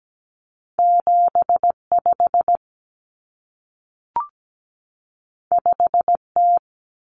Morse code, heard as a keyed tone, about 17 wpm, 700 Hz. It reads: 75 5T
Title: Common Cut Numbers - 2-Digit 17wpm